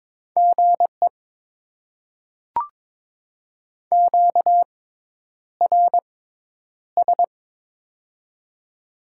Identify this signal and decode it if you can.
Morse code, heard as a keyed tone, about 22 wpm, spaced as for 10 wpm, 700 Hz. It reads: GE QRS
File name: Top 90 QSO Elements - 6x Character Spacing - 22wpm